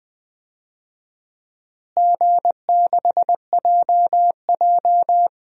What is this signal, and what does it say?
Morse code, heard as a keyed tone, about 20 wpm, 700 Hz. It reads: G6JJ
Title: DXCC-100 Call Signs - 4 Characters - Repeated 2x - 20wpm